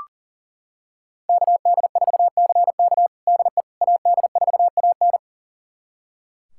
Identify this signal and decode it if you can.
Morse code, heard as a keyed tone, about 40 wpm, 700 Hz. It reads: KD4CK DE AD4AN
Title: USA Callsign DE Callsign - 5 Characters - 40wpm